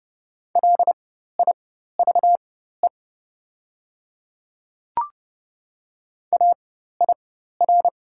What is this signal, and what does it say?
Morse code, heard as a keyed tone, about 30 wpm, spaced as for 14 wpm, 700 Hz. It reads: LIVE AIR
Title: 4x Character Spacing - Top 300 Words - 30wpm